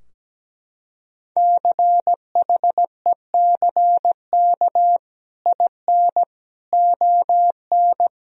Morse code, heard as a keyed tone, about 17 wpm, 700 Hz. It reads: CHECK IN ON
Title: Sentences from Top 1000 Words - Review Part 3 - 17wpm